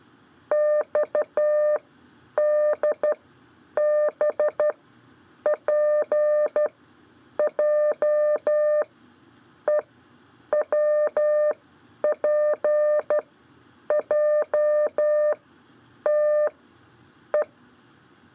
Morse code, heard as a keyed tone, about 12 wpm, spaced as for 8 wpm, 600 Hz. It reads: XDBPJEWPJTE
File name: 4 LEZIONE CW 1